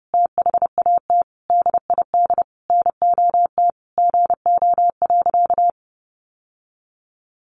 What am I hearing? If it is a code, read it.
Morse code, heard as a keyed tone, about 30 wpm, 700 Hz. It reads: THAT DID NOT GO.
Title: Sentences from Top 800 Words 30wpm